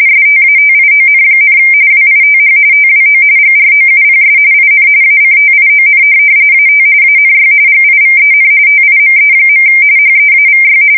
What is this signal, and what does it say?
RTTY, about 45.45 baud. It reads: STATION, WITH TWO GROUPS IN ACTION AT ANY GIVEN TIME. HEAVY
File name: fsk170 / USS-Massachusetts